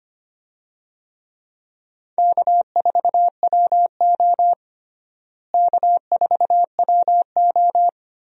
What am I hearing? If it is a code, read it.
Morse code, heard as a keyed tone, about 25 wpm, 700 Hz. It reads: K4WO K4WO
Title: USA Call Signs - 4 Characters - Repeated 2x - 25wpm